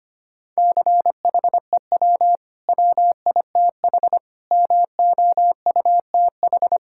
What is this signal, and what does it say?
Morse code, heard as a keyed tone, about 25 wpm, 700 Hz. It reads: CHEW WITH MOUTH
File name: Easy Sentences with Hints - 25wpm